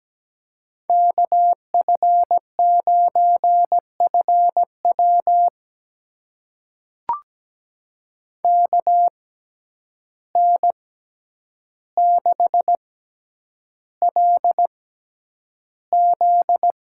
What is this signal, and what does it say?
Morse code, heard as a keyed tone, about 17 wpm, spaced as for 9 wpm, 700 Hz. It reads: KF9FW KN6LZ
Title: USA Call Signs - 5 Characters - 6x Character Spacing - 17wpm